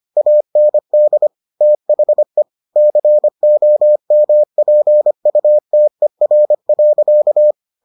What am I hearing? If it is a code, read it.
Morse code, heard as a keyed tone, about 25 wpm, 600 Hz. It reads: AND THE COMPUTER.